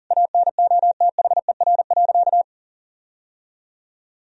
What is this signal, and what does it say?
Morse code, heard as a keyed tone, about 40 wpm, 700 Hz. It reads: ANOTHER.
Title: Sentences from Top 1000 Words - Review Part 1 - 40wpm